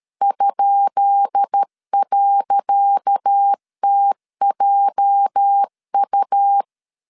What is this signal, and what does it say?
Morse code, heard as a keyed tone, about 13 wpm, 800 Hz. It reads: ?.TJU
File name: LEZIONI / 10 SIGNS / 10C DOT 12wpm 2451